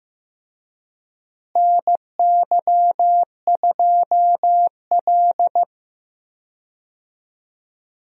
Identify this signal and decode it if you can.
Morse code, heard as a keyed tone, about 15 wpm, 700 Hz. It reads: NY2L
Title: USA Call Signs - 4 Characters - Repeated 3x - 15wpm